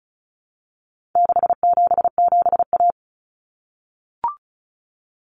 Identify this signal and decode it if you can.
Morse code, heard as a keyed tone, about 34 wpm, 700 Hz. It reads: 677A
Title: Common Cut Numbers - 4-Digit 35wpm